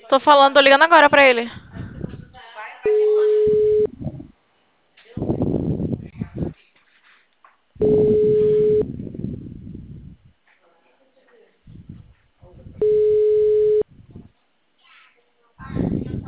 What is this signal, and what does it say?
Ringback tone (Europe, 425 Hz), with speech.
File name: out- -2012